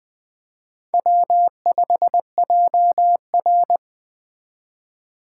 Morse code, heard as a keyed tone, about 19 wpm, 700 Hz. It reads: W5JR